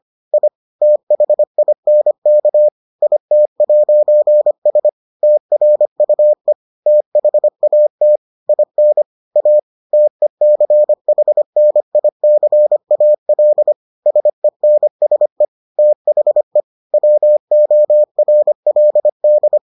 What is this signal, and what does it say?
Morse code, heard as a keyed tone, about 25 wpm, 600 Hz. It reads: I THINK IT'S TRUE THAT IN A TECHNICAL SENSE THE WORLD